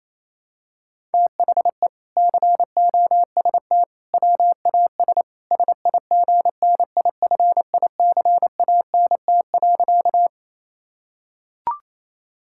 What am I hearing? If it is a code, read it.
Morse code, heard as a keyed tone, about 28 wpm, 700 Hz. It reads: THE COST WAS SIGNIFICANT.
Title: Sentences from Top 1000 Words - Review Part 6 - 28wpm